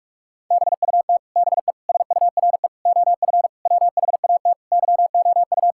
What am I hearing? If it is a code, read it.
Morse code, heard as a keyed tone, about 45 wpm, 700 Hz. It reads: BUT BE SURE OF WHAT YOU
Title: Sentences from Top 1600 Words 45wpm